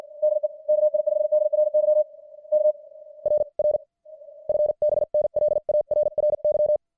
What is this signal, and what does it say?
Morse code, heard as a keyed tone, about 50 wpm, 600 Hz. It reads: DE KI5DRY K RR FB NL ARRY
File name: archived / CWBros